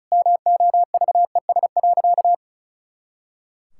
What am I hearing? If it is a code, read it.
Morse code, heard as a keyed tone, about 36 wpm, 700 Hz. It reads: MOVES.
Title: Pangram Challenge v2 - 35wpm